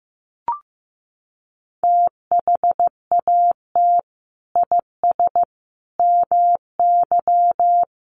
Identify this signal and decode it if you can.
Morse code, heard as a keyed tone, about 15 wpm, 700 Hz. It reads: THAT IS MY
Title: Sentences from Top 700 Words 15wpm